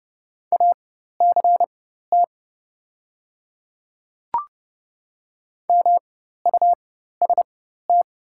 Morse code, heard as a keyed tone, about 30 wpm, spaced as for 14 wpm, 700 Hz. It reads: ACT MUST
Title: 4x Character Spacing - Top 200 Words - 30wpm